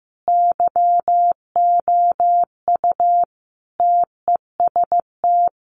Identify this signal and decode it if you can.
Morse code, heard as a keyed tone, about 15 wpm, 700 Hz. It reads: YOU TEST
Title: Sentences from Top 1100 Words 15wpm